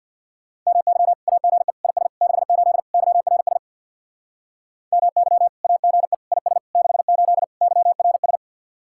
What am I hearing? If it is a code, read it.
Morse code, heard as a keyed tone, about 55 wpm, 700 Hz. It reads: MY AGE IS 68 YRS MY AGE IS 68 YRS
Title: Speed-Racing Common QSO Phrases - 50wpm